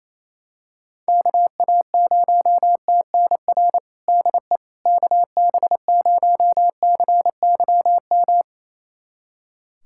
Morse code, heard as a keyed tone, about 28 wpm, 700 Hz. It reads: KA0TNR DE KB0CYM
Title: USA Callsign DE Callsign 28wpm